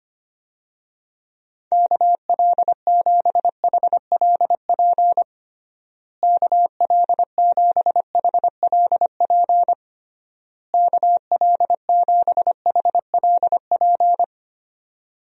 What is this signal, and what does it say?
Morse code, heard as a keyed tone, about 25 wpm, 700 Hz. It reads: KL7HLP KL7HLP KL7HLP
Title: DXCC-100 Call Signs - 6 Characters - Repeated 3x - 25wpm